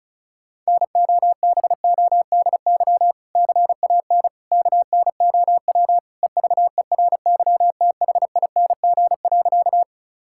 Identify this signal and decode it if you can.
Morse code, heard as a keyed tone, about 36 wpm, 700 Hz. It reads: NOBODY CAN KNOW EVERYTHING.
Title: Sentences from Top 1600 Words 35wpm